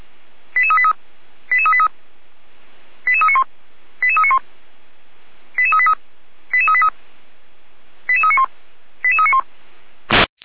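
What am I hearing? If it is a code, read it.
Five-tone selective call (ZVEI), heard as quick runs of five tones, tones of about 70 ms. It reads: 88383 88383 88391 88391 88383 88383 88391 88391
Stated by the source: Five-tone ZVEI selective call received by SDR.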